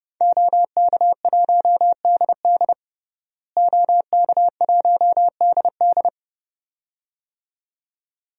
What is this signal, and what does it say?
Morse code, heard as a keyed tone, about 30 wpm, 700 Hz. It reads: OK1DD OK1DD
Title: DXCC-100 Call Signs - 5 Characters - Repeated 2x - 30wpm